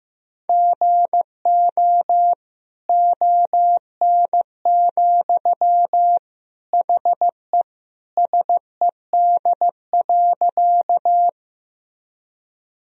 Morse code, heard as a keyed tone, about 15 wpm, 700 Hz. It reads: GO ON, HE SED.
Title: Sentences from Top 100 Words and Abbreviations 15wpm